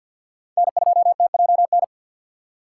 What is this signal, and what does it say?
Morse code, heard as a keyed tone, about 51 wpm, 700 Hz. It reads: N2TJN
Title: USA Call Signs - 5 Characters - 2x Character Spacing - 50wpm